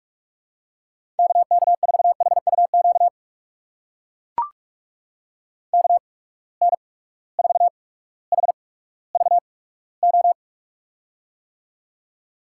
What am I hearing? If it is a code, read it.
Morse code, heard as a keyed tone, about 44 wpm, spaced as for 19 wpm, 700 Hz. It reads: KK4HUQ KN4HVO
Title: USA Call Signs - 6 Characters - 8x Character Spacing - 45wpm